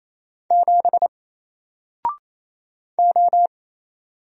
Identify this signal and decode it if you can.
Morse code, heard as a keyed tone, about 27 wpm, 700 Hz. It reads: 7 O
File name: Single-Letter-Number-28wpm